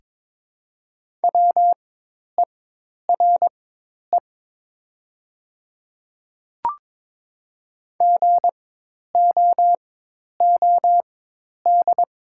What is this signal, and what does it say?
Morse code, heard as a keyed tone, about 22 wpm, spaced as for 10 wpm, 700 Hz. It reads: WERE GOOD
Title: 4x Character Spacing - Top 200 Words - 22wpm